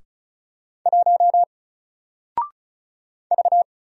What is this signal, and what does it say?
Morse code, heard as a keyed tone, about 36 wpm, 700 Hz. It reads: 1 V
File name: Single-Letter-Number-35wpm